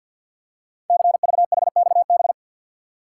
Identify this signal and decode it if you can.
Morse code, heard as a keyed tone, about 48 wpm, 700 Hz. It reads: CVHXB
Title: Code Groups - 5 Letters - 50wpm